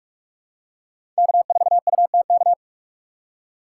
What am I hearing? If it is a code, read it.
Morse code, heard as a keyed tone, about 46 wpm, 700 Hz. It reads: K4UTK
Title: USA Call Signs - 5 Characters - 3x Character Spacing - 45wpm